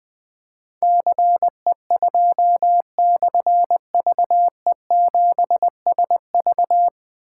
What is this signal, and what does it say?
Morse code, heard as a keyed tone, about 21 wpm, 700 Hz. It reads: CE2/VE7SV
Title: Difficult Call Signs 20wpm